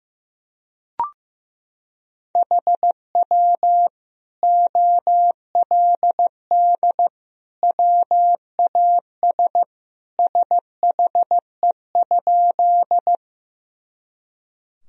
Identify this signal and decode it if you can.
Morse code, heard as a keyed tone, about 15 wpm, 700 Hz. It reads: HW OLD WAS SHE?
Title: Sentences from Top 200 Words and Abbreviations 15wpm